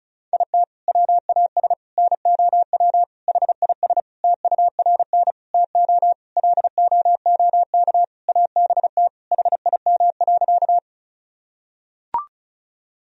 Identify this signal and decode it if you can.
Morse code, heard as a keyed tone, about 34 wpm, 700 Hz. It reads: IT WAS NOW HIS TURN TO LOOK ABT HIM.
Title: Sentences from Top 200 Words and Abbreviations 35wpm